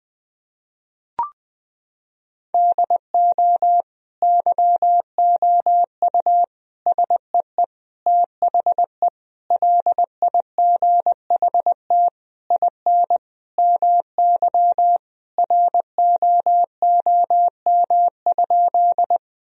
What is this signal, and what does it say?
Morse code, heard as a keyed tone, about 20 wpm, 700 Hz. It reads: DO YOU SEE THE LIGHT IN MY ROOM?
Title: Sentences from Top 1000 Words - Review Part 1 - 20wpm